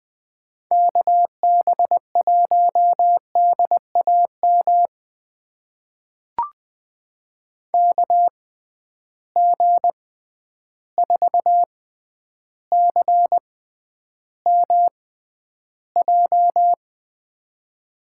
Morse code, heard as a keyed tone, about 20 wpm, spaced as for 10 wpm, 700 Hz. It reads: KB1DAM KG4CMJ
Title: USA Call Signs - 6 Characters - 6x Character Spacing - 20wpm